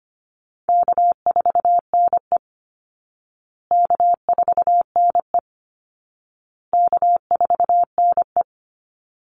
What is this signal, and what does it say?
Morse code, heard as a keyed tone, about 25 wpm, 700 Hz. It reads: K4NE K4NE K4NE
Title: Easy Call Signs Repeated 3x - 25wpm